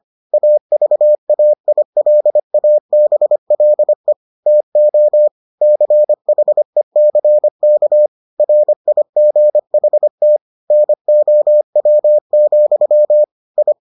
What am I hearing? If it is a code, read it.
Morse code, heard as a keyed tone, about 25 wpm, 600 Hz. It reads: AVAILABLE TO CHECK RIGHT NOW, I